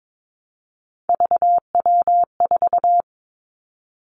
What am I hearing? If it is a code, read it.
Morse code, heard as a keyed tone, about 22 wpm, 700 Hz. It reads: VW4
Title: Three Challenging Letter-Number Combinations 22wpm